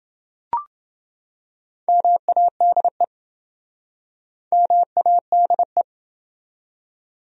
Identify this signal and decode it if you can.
Morse code, heard as a keyed tone, about 28 wpm, 700 Hz. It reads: MADE MADE